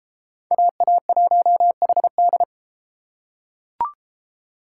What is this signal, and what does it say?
Morse code, heard as a keyed tone, about 30 wpm, 700 Hz. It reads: AA1HD